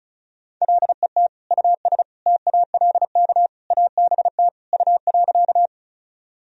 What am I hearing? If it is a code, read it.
Morse code, heard as a keyed tone, about 34 wpm, 700 Hz. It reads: LET US TALK ABT U.
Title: Sentences from Top 400 Words and Abbreviations 35wpm